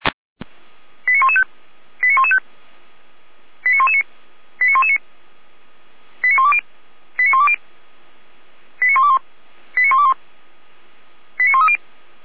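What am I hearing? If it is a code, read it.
Five-tone selective call (ZVEI), heard as quick runs of five tones, tones of about 70 ms. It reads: 89115 89115 89119 89119 89120 89120 89121 89121 89130
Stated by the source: ZVEI five-tone selective call, address 89078, received by SDR.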